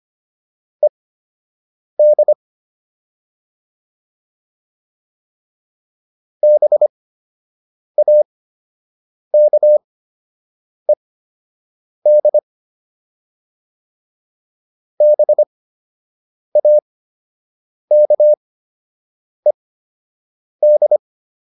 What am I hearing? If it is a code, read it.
Morse code, heard as a keyed tone, about 25 wpm, spaced as for 7 wpm, 600 Hz. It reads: ED BAKED BAKED